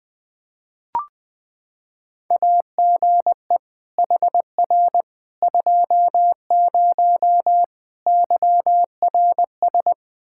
Morse code, heard as a keyed tone, about 21 wpm, 700 Hz. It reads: AGE HR 20 YRS
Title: K1USN SST QSO Practice 20wpm